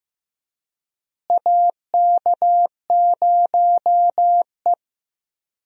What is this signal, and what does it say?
Morse code, heard as a keyed tone, about 15 wpm, 700 Hz. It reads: AK0E
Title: USA Call Signs - 4 Characters - Repeated 3x - 15wpm